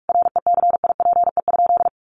Morse code, heard as a keyed tone, about 44 wpm, 700 Hz. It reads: RECIPE?
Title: Sentences from Top 1200 Words 45wpm